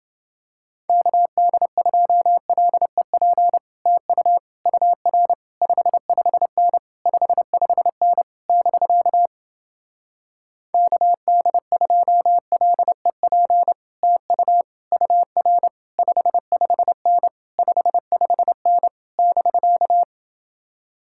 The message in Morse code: KD2LEP TU UR 55N 55N <BK> KD2LEP TU UR 55N 55N <BK>

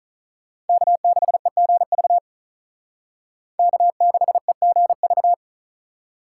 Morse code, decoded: K6EGV K6EGV

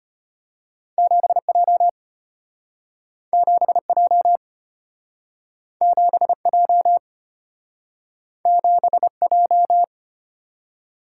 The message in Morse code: 7J 7J 7J 7J

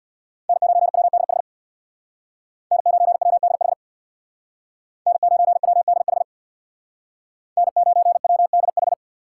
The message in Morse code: N9WDH N9WDH N9WDH N9WDH

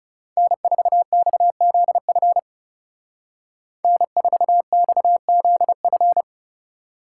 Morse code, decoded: N4XZF N4XZF

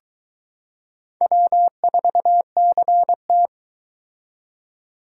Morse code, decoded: W4CT

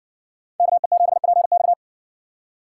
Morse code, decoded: KE7PX

